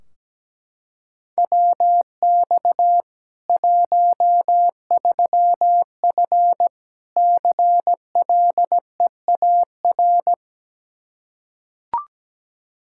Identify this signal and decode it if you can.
Morse code, heard as a keyed tone, about 17 wpm, 700 Hz. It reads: WX 13F CLEAR
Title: K1USN SST QSO Practice 17wpm